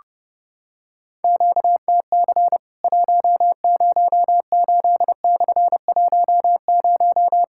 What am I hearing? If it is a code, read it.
Morse code, heard as a keyed tone, about 30 wpm, 700 Hz. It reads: QTC 108/10
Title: Worked All Europe DX Contest - QTC Exchange - 30wpm